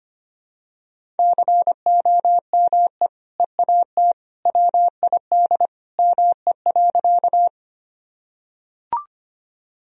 Morse code, heard as a keyed tone, about 25 wpm, 700 Hz. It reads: COME EAT WID ME.
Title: Sentences from Top 300 Words and Abbreviations 25wpm